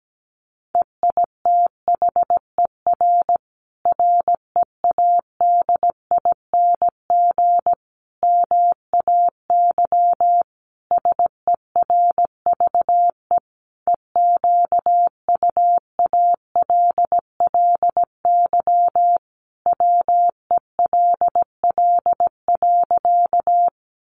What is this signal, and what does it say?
Morse code, heard as a keyed tone, about 17 wpm, 700 Hz. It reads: EITHER READING MAY SERVE EQUALLY WELL.
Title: Sentences from Top 1400 Words 17wpm